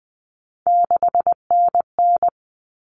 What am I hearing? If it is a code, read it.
Morse code, heard as a keyed tone, about 20 wpm, 700 Hz. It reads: 6NN
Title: Common Cut Numbers - 3-Digit 20wpm